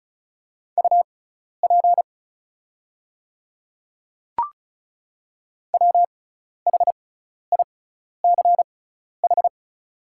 Morse code, heard as a keyed tone, about 35 wpm, spaced as for 12 wpm, 690 Hz. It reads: UP WHICH
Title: 6x Character Spacing - Top 100 Words - 35wpm